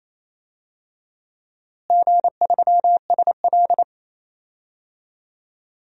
Morse code, heard as a keyed tone, about 28 wpm, 700 Hz. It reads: G3SL
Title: DXCC-100 Call Signs - 4 Characters - Repeated 2x - 28wpm